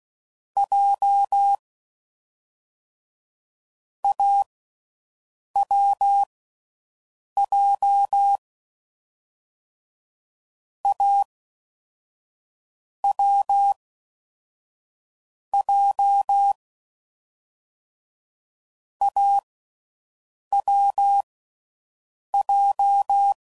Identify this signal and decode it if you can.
Morse code, heard as a keyed tone, about 16 wpm, 800 Hz. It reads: J A W J A W J A W J
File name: Combinations / Difficult Combos / AWJ1